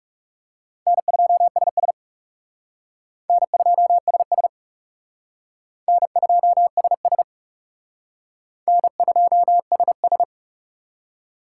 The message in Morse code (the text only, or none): N2SS N2SS N2SS N2SS